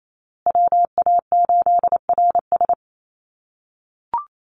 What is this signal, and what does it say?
Morse code, heard as a keyed tone, about 28 wpm, 700 Hz. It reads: WA8RS
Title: USA Call Signs Repeated 3x 28wpm